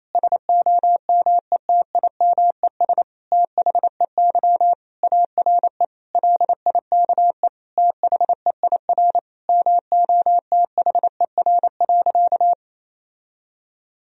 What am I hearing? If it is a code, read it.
Morse code, heard as a keyed tone, about 28 wpm, 700 Hz. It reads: SOMETIMES THEY ARE LIKE THEIR MOTHER.
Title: Sentences from Top 1000 Words - Review Part 3 - 28wpm